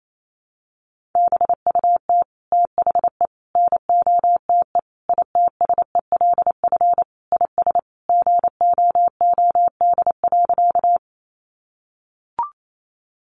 Morse code, heard as a keyed tone, about 28 wpm, 700 Hz. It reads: BUT THE NOTE ITSELF IS GOOD.
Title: Sentences from Top 800 Words 28wpm